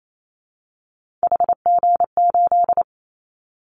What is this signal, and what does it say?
Morse code, heard as a keyed tone, about 28 wpm, 700 Hz. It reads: HG8
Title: Callsign Prefixes for Top 25 DXCC - 28wpm